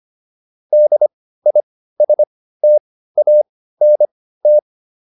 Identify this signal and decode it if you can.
Morse code, heard as a keyed tone, about 25 wpm, spaced as for 15 wpm, 600 Hz. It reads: DISTANT